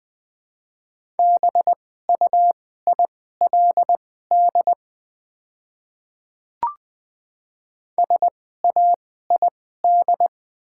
Morse code, heard as a keyed tone, about 20 wpm, spaced as for 14 wpm, 700 Hz. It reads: BUILD SAID
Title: 2x Character Spacing - Top 200 Words - 20wpm